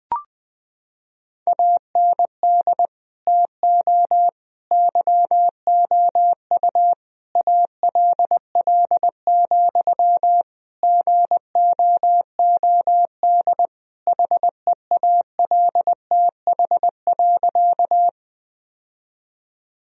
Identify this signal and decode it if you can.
Morse code, heard as a keyed tone, about 20 wpm, 700 Hz. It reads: AND TO YOU ALL, GOOD HEALTH.
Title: Sentences from Top 1000 Words - Review Part 3 - 20wpm